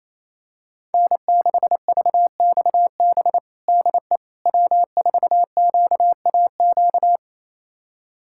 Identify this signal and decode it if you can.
Morse code, heard as a keyed tone, about 28 wpm, 700 Hz. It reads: N6VXB DE W4QAQ
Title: USA Callsign DE Callsign - 5 Characters - 28wpm